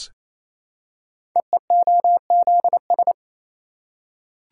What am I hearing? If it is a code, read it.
Morse code, heard as a keyed tone, about 28 wpm, 700 Hz. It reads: EEOZS
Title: Code Groups - 5 Letters - 28wpm